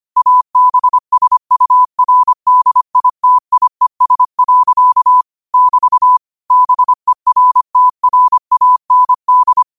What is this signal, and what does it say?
Morse code, heard as a keyed tone, about 25 wpm, 1000 Hz. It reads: ABSURDITIES. <BT> BERTRAND
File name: qotdfiles / 25wpm